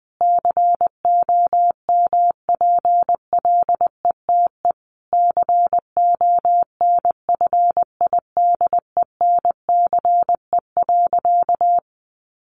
Morse code, heard as a keyed tone, about 19 wpm, 700 Hz. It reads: COMPLETE CONFIDENCE.
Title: Sentences from Top 1500 Words 20wpm